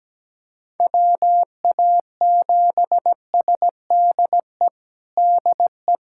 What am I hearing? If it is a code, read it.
Morse code, heard as a keyed tone, about 17 wpm, 700 Hz. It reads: WA7SDE DE
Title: USA Callsign DE Callsign - 6 Characters - 17wpm